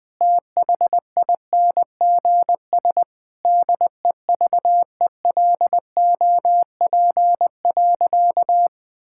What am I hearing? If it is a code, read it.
Morse code, heard as a keyed tone, about 20 wpm, 700 Hz. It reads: THINGS DEVELOP.
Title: Sentences with 6-Word Phrases - Spoken Beginning - 20wpm